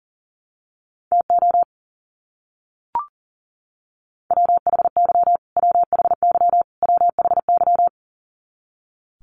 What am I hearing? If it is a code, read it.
Morse code, heard as a keyed tone, about 40 wpm, 700 Hz. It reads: TO WHY WHY WHY
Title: Top 500 Words Repeated 40wpm